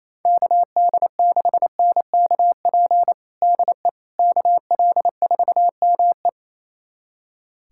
Morse code, heard as a keyed tone, about 28 wpm, 700 Hz. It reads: KD6NKP DE KL4ME
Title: USA Callsign DE Callsign 28wpm